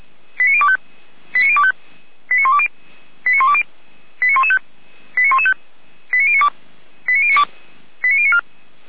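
Five-tone selective call (ZVEI), heard as quick runs of five tones, tones of about 80 ms. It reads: 89025 89025 89120 89120 89115 89115 89092 89092 89094
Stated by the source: ZVEI five-tone selective call received by SDR.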